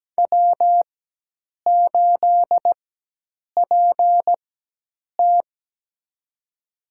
Morse code, heard as a keyed tone, about 17 wpm, spaced as for 8 wpm, 700 Hz. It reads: W8PT
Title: USA Call Signs - 4 Characters - 4x Character Spacing - 17wpm